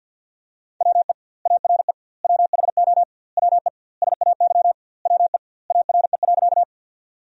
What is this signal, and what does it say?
Morse code, heard as a keyed tone, about 49 wpm, 700 Hz. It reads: WE ARE WHO WE SAY WE ARE.
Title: 2x Word-Spacing - Sentences from Top 200 Words - 50wpm